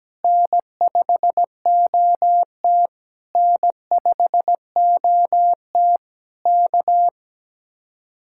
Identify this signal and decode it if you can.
Morse code, heard as a keyed tone, about 17 wpm, 700 Hz. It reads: N5OT N5OT K
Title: Speed-Racing Common QSO Phrases - 17wpm